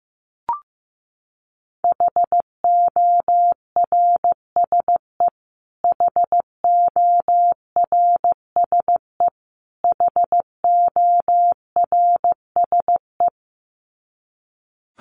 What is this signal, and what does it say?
Morse code, heard as a keyed tone, about 15 wpm, 700 Hz. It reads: HORSE HORSE HORSE
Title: Top 300 Words Repeated 15wpm